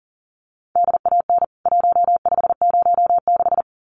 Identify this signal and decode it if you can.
Morse code, heard as a keyed tone, about 37 wpm, 700 Hz. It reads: DAN 1506
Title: CWT Exchange 40wpm